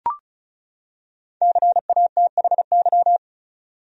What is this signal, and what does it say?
Morse code, heard as a keyed tone, about 34 wpm, 700 Hz. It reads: CATHY
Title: USA Common Names - 35wpm